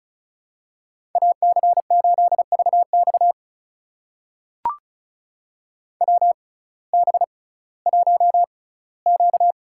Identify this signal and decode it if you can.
Morse code, heard as a keyed tone, about 35 wpm, spaced as for 19 wpm, 700 Hz. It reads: AC8VX WB1Q
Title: USA Call Signs - 6x Character Spacing - 35wpm